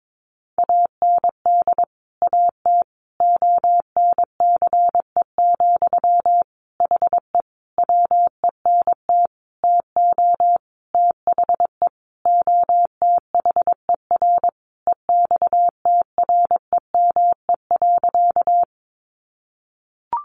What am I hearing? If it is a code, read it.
Morse code, heard as a keyed tone, about 22 wpm, 700 Hz. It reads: AND AT ONCE, HE WENT TO THE OTHER EXTREME.
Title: Sentences from Top 1300 Words 22wpm